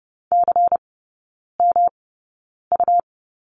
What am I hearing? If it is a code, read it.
Morse code, heard as a keyed tone, about 30 wpm, spaced as for 9 wpm, 700 Hz. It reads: CMU
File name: Lesson 19e - Letter w - Three Characters 30wpm